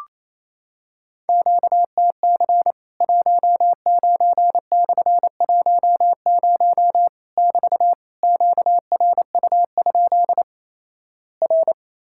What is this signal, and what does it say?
Morse code, heard as a keyed tone, about 28 wpm, 700 Hz. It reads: QTC 19/10 <BT> QRU? R
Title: Worked All Europe DX Contest - QTC Exchange - 28wpm